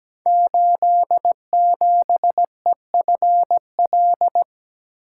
Morse code, heard as a keyed tone, about 17 wpm, 700 Hz. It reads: 87EFL
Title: Code Groups - 5 Letter:Numbers - 17wpm